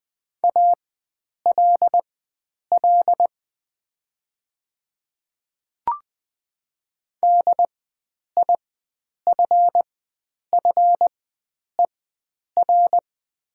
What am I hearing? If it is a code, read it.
Morse code, heard as a keyed tone, about 20 wpm, spaced as for 9 wpm, 700 Hz. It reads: ALL DIFFER
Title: 4x Character Spacing - Top 500 Words - 20wpm